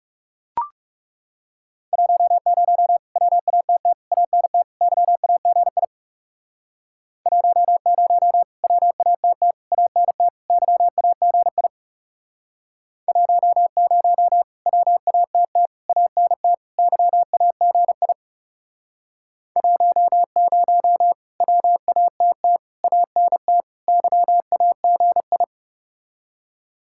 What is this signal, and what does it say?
Morse code, heard as a keyed tone, about 40 wpm, 700 Hz. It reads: 10 WATT ANT YAGI 10 WATT ANT YAGI 10 WATT ANT YAGI 10 WATT ANT YAGI